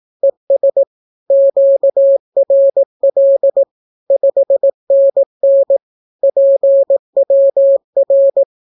Morse code, heard as a keyed tone, about 18 wpm, 550 Hz. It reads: ES QRL 5NN PWR
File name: naukapodstaw / skroty / skroty